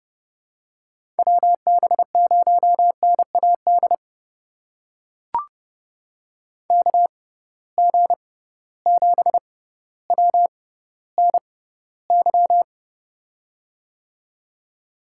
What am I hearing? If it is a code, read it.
Morse code, heard as a keyed tone, about 32 wpm, spaced as for 16 wpm, 700 Hz. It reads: WB0NAD KG7WNY
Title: USA Call Signs - 6 Characters - 6x Character Spacing - 30wpm